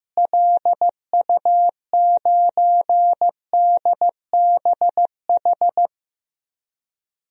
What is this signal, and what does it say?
Morse code, heard as a keyed tone, about 15 wpm, 700 Hz. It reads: LU9DBH
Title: DXCC-100 Call Signs - 6 Characters - Repeated 2x - 15wpm